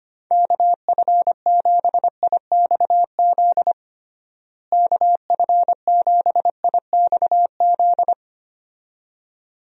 Morse code, heard as a keyed tone, about 25 wpm, 700 Hz. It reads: KF7IXZ KF7IXZ
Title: USA Call Signs - 6 Characters - Repeated 2x - 25wpm